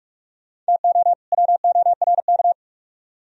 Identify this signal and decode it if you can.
Morse code, heard as a keyed tone, about 55 wpm, 700 Hz. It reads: TO WORK
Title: Speed-Racing - Sets of 2 Words 45wpm